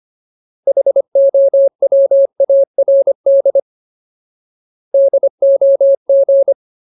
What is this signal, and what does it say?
Morse code, heard as a keyed tone, about 25 wpm, 550 Hz. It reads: HOWARD DOG
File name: Ultimate Word List / 25WPM Ultimate